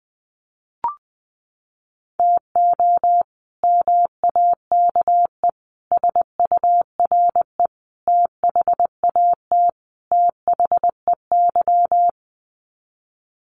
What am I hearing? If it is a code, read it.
Morse code, heard as a keyed tone, about 20 wpm, 700 Hz. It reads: TO MAKE SURE THAT THEY
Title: Sets of 5 Words Encore2 20wpm